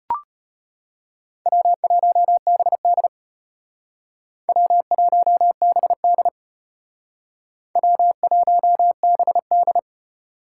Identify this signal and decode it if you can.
Morse code, heard as a keyed tone, about 32 wpm, 700 Hz. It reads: W1BD W1BD W1BD